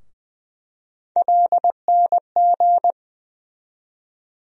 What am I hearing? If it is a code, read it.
Morse code, heard as a keyed tone, about 20 wpm, 700 Hz. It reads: LNG
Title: USA Call Sign Postfixes - 3 Letters 20wpm